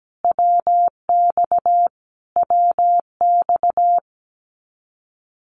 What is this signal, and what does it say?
Morse code, heard as a keyed tone, about 17 wpm, 700 Hz. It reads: WX WX
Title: Abbreviations Repeated 3x - 17wpm